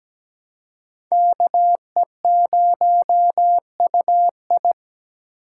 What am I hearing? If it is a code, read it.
Morse code, heard as a keyed tone, about 20 wpm, 700 Hz. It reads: KE0UI
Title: Speed Racing - USA Call Signs 17wpm